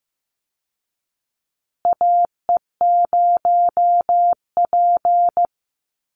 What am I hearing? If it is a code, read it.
Morse code, heard as a keyed tone, about 15 wpm, 700 Hz. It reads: AE0P
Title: Call Signs Repeated 3x - 15wpm